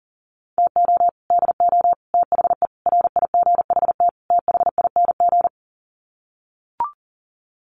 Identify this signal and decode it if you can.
Morse code, heard as a keyed tone, about 40 wpm, 700 Hz. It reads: TO DO THE RIGHT THING
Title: Sets of 5 Words 40wpm